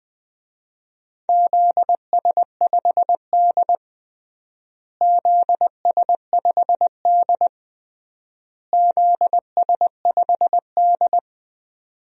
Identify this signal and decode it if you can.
Morse code, heard as a keyed tone, about 20 wpm, 700 Hz. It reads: ZS5D ZS5D ZS5D
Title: DXCC-100 Call Signs - 4 Characters - Repeated 3x - 20wpm